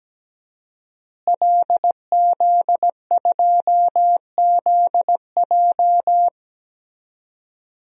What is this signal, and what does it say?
Morse code, heard as a keyed tone, about 17 wpm, 700 Hz. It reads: LZ2ZJ
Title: DXCC-100 Call Signs - 5 Characters - Repeated 2x - 17wpm